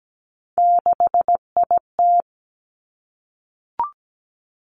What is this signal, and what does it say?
Morse code, heard as a keyed tone, about 17 wpm, 700 Hz. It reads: 6IT
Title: Three Challenging Letter-Number Combinations 17wpm